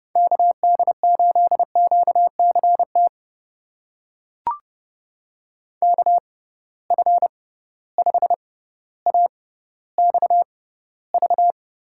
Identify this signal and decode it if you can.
Morse code, heard as a keyed tone, about 30 wpm, spaced as for 15 wpm, 700 Hz. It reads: KD8QCT KF5AXV
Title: USA Call Signs - 6 Characters - 6x Character Spacing - 30wpm